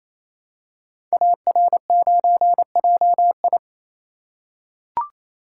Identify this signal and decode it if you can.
Morse code, heard as a keyed tone, about 27 wpm, 700 Hz. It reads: AR9JI
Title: Code Groups - 5 Letter:Numbers - 28wpm